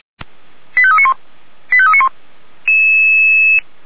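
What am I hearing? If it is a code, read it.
Five-tone selective call (ZVEI), heard as quick runs of five tones, tones of about 70 ms. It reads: 85391 85391
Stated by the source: ZVEI five-tone selective call received on an SDR.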